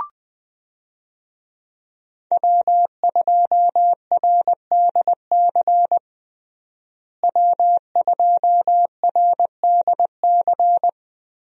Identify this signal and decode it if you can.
Morse code, heard as a keyed tone, about 20 wpm, 700 Hz. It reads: W2RDC W2RDC
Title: USA Call Signs - 5 Characters - Repeated 3x - 20wpm